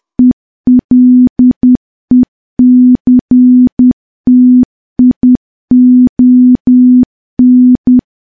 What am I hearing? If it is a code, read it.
Morse code, heard as a keyed tone, about 10 wpm, 250 Hz. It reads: ELECTION
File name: morse / morse-training-10WPM